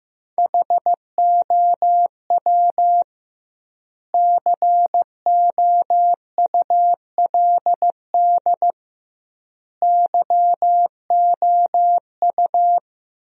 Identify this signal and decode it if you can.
Morse code, heard as a keyed tone, about 15 wpm, 700 Hz. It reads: HOW COULD YOU
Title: 2x Word-Spacing - Sentences from Top 300 Words - 15wpm